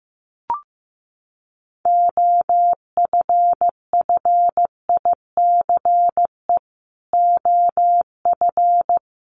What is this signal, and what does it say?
Morse code, heard as a keyed tone, about 14 wpm, 700 Hz. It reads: OFFICE OF
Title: Sets of 5 Words Encore2 15wpm